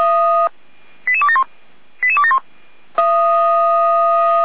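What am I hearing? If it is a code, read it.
Five-tone selective call (ZVEI), heard as quick runs of five tones, tones of about 80 ms. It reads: 88271 88271
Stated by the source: ZVEI five-tone selective call received on an SDR, from a fire brigade's alerting recordings.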